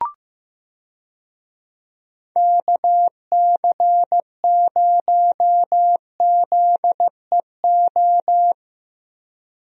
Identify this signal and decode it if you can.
Morse code, heard as a keyed tone, about 15 wpm, 700 Hz. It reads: KC0ZEO
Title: USA Call Signs - 6 Characters - Repeated 2x - 15wpm